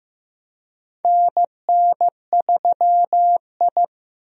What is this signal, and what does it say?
Morse code, heard as a keyed tone, about 15 wpm, 700 Hz. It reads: NN3I
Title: USA Call Signs - 4 Characters - Repeated 3x - 15wpm